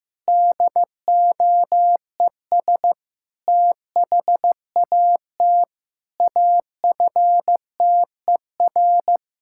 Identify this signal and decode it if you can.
Morse code, heard as a keyed tone, about 15 wpm, 700 Hz. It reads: DOES THAT AFTER
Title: Sentences from Top 1000 Words - Review Part 5 - 15wpm